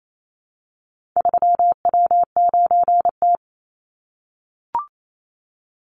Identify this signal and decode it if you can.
Morse code, heard as a keyed tone, about 28 wpm, 700 Hz. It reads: 3W9T
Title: Call Signs - 28wpm